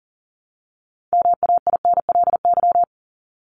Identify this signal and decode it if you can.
Morse code, heard as a keyed tone, about 40 wpm, 700 Hz. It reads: MAINLY
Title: Top 1251-1500 Words 40wpm